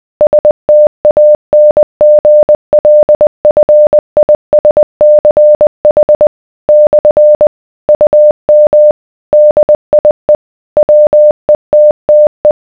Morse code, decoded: STANGLFISCH / UM DIE WETTE